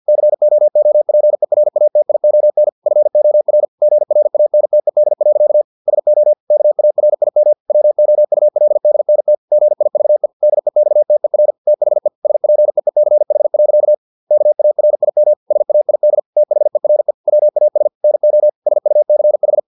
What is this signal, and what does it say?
Morse code, heard as a keyed tone, about 50 wpm, 600 Hz. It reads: COOPERATION FOR GRANTED. SO KARIM WOULDNT GIVE DEXTER THE SPEECH. KARIM SAID THERE WAS NO SUCH
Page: Never 49wpm